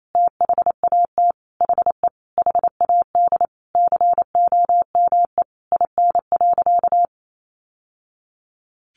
Morse code, heard as a keyed tone, about 28 wpm, 700 Hz. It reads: THAT HE HAD COME IN.
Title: Sentences from Top 1000 Words 28wpm